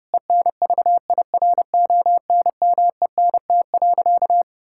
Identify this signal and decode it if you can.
Morse code, heard as a keyed tone, about 32 wpm, 700 Hz. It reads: ENVIRONMENT.
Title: Sentences from Top 1000 Words - Review Part 6 - 30wpm